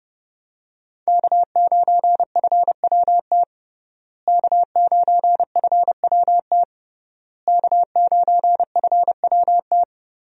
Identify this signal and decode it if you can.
Morse code, heard as a keyed tone, about 30 wpm, 700 Hz. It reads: K9FWT K9FWT K9FWT
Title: USA Call Signs - 5 Characters - Repeated 3x - 30wpm